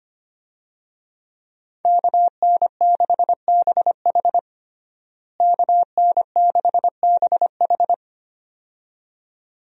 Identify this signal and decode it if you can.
Morse code, heard as a keyed tone, about 25 wpm, 700 Hz. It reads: KN6BH KN6BH
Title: USA Call Signs - 5 Characters - Repeated 2x - 25wpm